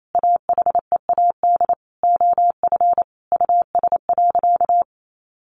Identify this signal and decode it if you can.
Morse code, heard as a keyed tone, about 28 wpm, 700 Hz. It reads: AHEAD OF US.
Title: Sentences from Top 1300 Words 28wpm